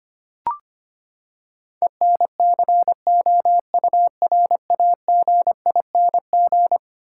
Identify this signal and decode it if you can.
Morse code, heard as a keyed tone, about 25 wpm, 700 Hz. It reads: ENCOURAGING
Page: morse-code-ninja-practice-25wpm